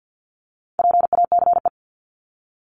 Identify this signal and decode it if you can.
Morse code, heard as a keyed tone, about 50 wpm, 700 Hz. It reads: PACE
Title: Top 1751-2000 Words 50wpm